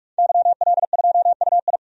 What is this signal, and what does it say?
Morse code, heard as a keyed tone, about 46 wpm, 700 Hz. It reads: YL2UI
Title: DXCC-100 Call Signs - 5 Characters - 45wpm